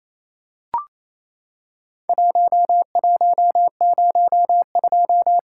Morse code, heard as a keyed tone, about 28 wpm, 700 Hz. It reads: 1102
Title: Worked All Europe DX Contest - QTC Exchange - 28wpm